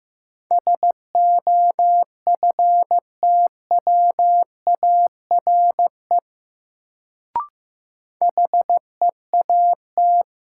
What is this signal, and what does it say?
Morse code, heard as a keyed tone, about 16 wpm, 700 Hz. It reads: SOFTWARE HEAT
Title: Sentences from Top 1000 Words - Review Part 3 - New Word Drill - 15wpm